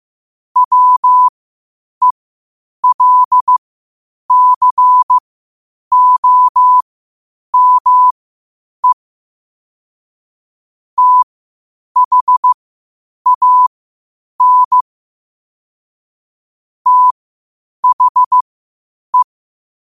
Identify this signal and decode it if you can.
Morse code, heard as a keyed tone, about 15 wpm, spaced as for 8 wpm, 1000 Hz. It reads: WELCOME THAN THE
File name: qotdfiles / 8wpm